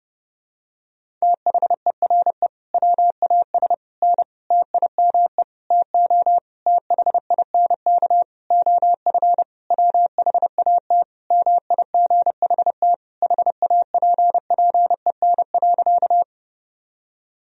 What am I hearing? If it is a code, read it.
Morse code, heard as a keyed tone, about 30 wpm, 700 Hz. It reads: THERE WAS N TIME TO THINK OF WHAT MIGHT HAPPEN.
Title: Sentences from Top 400 Words and Abbreviations 30wpm